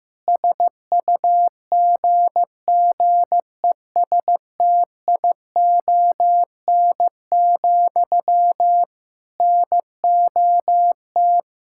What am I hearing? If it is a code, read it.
Morse code, heard as a keyed tone, about 16 wpm, 700 Hz. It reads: SUGGESTION, NOT
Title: Sentences from Top 1600 Words 15wpm